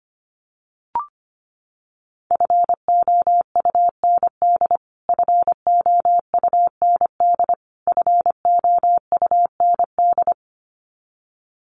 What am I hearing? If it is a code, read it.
Morse code, heard as a keyed tone, about 25 wpm, 700 Hz. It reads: FOUND FOUND FOUND
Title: Top 500 Words Repeated 25wpm